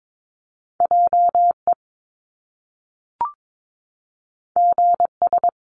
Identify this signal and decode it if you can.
Morse code, heard as a keyed tone, about 22 wpm, 700 Hz. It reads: JE GS
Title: Top Two Letter Combinations in Words 22wpm